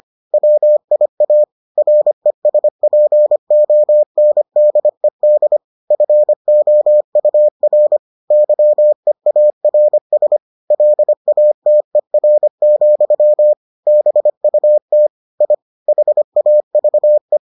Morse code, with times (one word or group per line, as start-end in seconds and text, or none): 0.3-1.4 WIA
1.8-5.6 RESPONDED
5.9-8.0 FOUR
8.3-10.4 YEARS
10.7-13.5 LATER,
13.9-15.1 BUT
15.4-15.5 I
15.9-17.4 HAVE